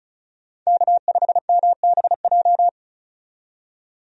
Morse code, heard as a keyed tone, about 36 wpm, 700 Hz. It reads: K5MBJ
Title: USA Call Signs - 5 Characters - 35wpm